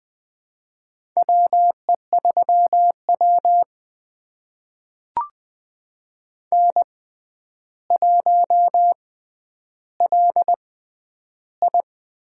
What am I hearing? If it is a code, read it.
Morse code, heard as a keyed tone, about 21 wpm, spaced as for 10 wpm, 700 Hz. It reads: WE3W N1LI
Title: USA Call Signs - 4 Characters - 6x Character Spacing - 20wpm